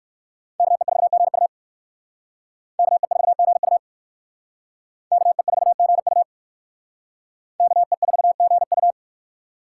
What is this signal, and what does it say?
Morse code, heard as a keyed tone, about 56 wpm, 700 Hz. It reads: KE4GU KE4GU KE4GU KE4GU
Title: Speed Racing - USA Call Signs 45wpm